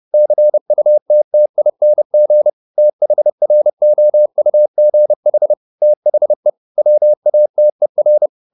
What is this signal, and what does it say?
Morse code, heard as a keyed tone, about 30 wpm, 600 Hz. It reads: CUTTING THROUGH THE WATER
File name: the old man and the sea / the old man and the sea 1 30wpm